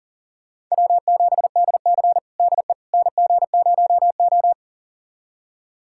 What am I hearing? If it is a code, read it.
Morse code, heard as a keyed tone, about 40 wpm, 700 Hz. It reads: W7DC DE NG0O
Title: USA Callsign DE Callsign - 4 Characters - 40wpm